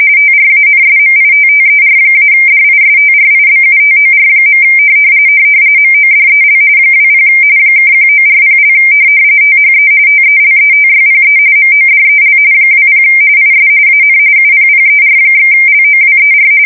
RTTY, about 45.45 baud. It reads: TRANSMITTER-DISTRIBUTOR. MODEL 20 (1940) THE TELETYPE MODEL 20 IS AN UPPER/LOWER
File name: fsk170 / teletype-corp